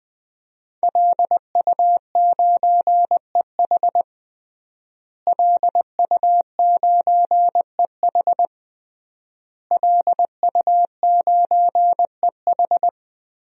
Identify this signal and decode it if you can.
Morse code, heard as a keyed tone, about 20 wpm, 700 Hz. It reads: LU9EH LU9EH LU9EH
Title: DXCC-100 Call Signs - 5 Characters - Repeated 3x - 20wpm